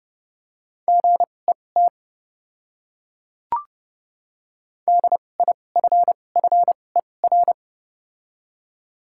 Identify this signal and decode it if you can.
Morse code, heard as a keyed tone, about 30 wpm, spaced as for 22 wpm, 700 Hz. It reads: GET DIFFER
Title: 2x Character Spacing - Top 200 Words - 30wpm